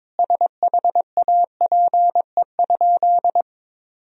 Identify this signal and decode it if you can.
Morse code, heard as a keyed tone, about 22 wpm, 700 Hz. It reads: SHAPE?
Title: Sentences from Top 1000 Words - Review Part 3 - 22wpm